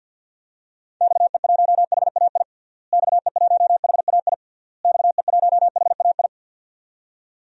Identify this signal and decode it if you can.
Morse code, heard as a keyed tone, about 51 wpm, 700 Hz. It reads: XE1HAI XE1HAI XE1HAI
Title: DXCC-100 Call Signs - 6 Characters - Repeated 3x - 50wpm